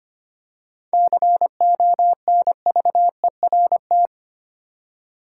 Morse code, heard as a keyed tone, about 25 wpm, 700 Hz. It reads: CONVERT
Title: Seven Letter Words - 25wpm